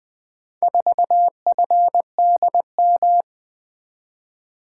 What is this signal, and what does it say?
Morse code, heard as a keyed tone, about 20 wpm, 700 Hz. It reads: 4FDM
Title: Four Letter-Number Combinations - 20wpm